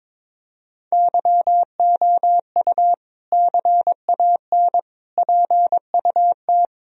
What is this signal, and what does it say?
Morse code, heard as a keyed tone, about 21 wpm, 700 Hz. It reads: YOU CAN PUT
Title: Sentences from Top 1000 Words - Review Part 2 - 22wpm